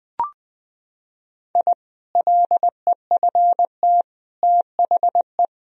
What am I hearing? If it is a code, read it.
Morse code, heard as a keyed tone, about 20 wpm, 700 Hz. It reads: I LEFT THE
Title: Sentences from Top 1000 Words - Review Part 2 - 20wpm